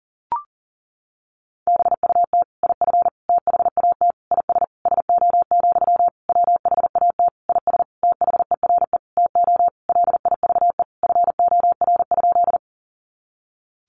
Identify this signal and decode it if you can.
Morse code, heard as a keyed tone, about 37 wpm, 700 Hz. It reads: BUT IF THAT IS SO, WHAT IS THERE TO LIVE FOR?
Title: Sentences from Top 200 Words 40wpm